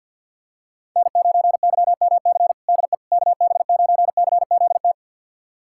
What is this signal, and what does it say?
Morse code, heard as a keyed tone, about 52 wpm, 700 Hz. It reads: N9YMC DE KB9CZT
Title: USA Callsign DE Callsign 50wpm